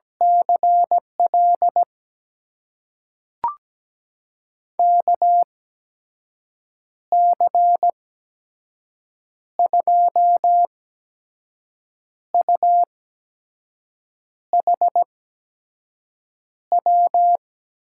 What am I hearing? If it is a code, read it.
Morse code, heard as a keyed tone, about 17 wpm, spaced as for 5 wpm, 700 Hz. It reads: CL KC2UHW
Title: USA Call Signs - 6 Characters - 8x Character Spacing - 17wpm